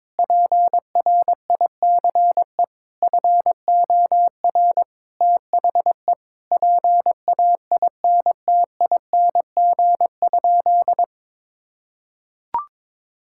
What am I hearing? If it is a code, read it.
Morse code, heard as a keyed tone, about 22 wpm, 700 Hz. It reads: PRICE FOR THE PAINTING?
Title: Sentences from Top 1000 Words - Review Part 6 - 22wpm